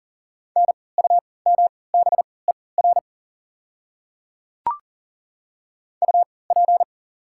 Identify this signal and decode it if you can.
Morse code, heard as a keyed tone, about 39 wpm, spaced as for 22 wpm, 690 Hz. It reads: NUMBER UP
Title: 3x Character Spacing - Top 100 Words - 40wpm